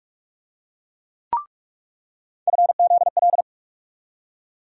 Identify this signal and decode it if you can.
Morse code, heard as a keyed tone, about 41 wpm, 700 Hz. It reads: FZL
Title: USA Call Sign Postfixes - 3 Letters 45wpm